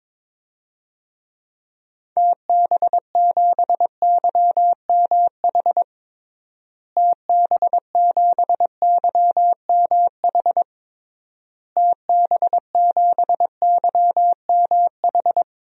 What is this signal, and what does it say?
Morse code, heard as a keyed tone, about 21 wpm, 700 Hz. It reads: TB7YMH TB7YMH TB7YMH
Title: DXCC-100 Call Signs - 6 Characters - Repeated 3x - 22wpm